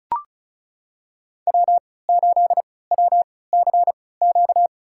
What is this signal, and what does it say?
Morse code, heard as a keyed tone, about 34 wpm, spaced as for 20 wpm, 700 Hz. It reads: W8WCQ
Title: USA Call Signs - 5 Characters - 3x Character Spacing - 35wpm